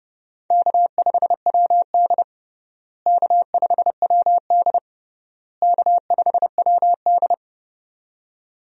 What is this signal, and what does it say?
Morse code, heard as a keyed tone, about 30 wpm, 700 Hz. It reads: K5WD K5WD K5WD
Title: USA Call Signs - 4 Characters - Repeated 3x - 30wpm